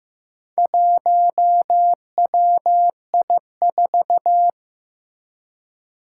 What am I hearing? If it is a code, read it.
Morse code, heard as a keyed tone, about 15 wpm, 700 Hz. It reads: 1WI4
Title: Four Challenging Letter-Number Combinations 15wpm